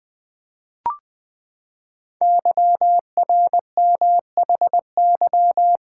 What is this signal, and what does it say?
Morse code, heard as a keyed tone, about 20 wpm, 700 Hz. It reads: YRMHY
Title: Code Groups - 5 Letters - 20wpm